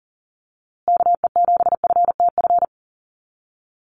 Morse code, heard as a keyed tone, about 40 wpm, 700 Hz. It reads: KE7FTF
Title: USA Call Signs Repeated 2x 40wpm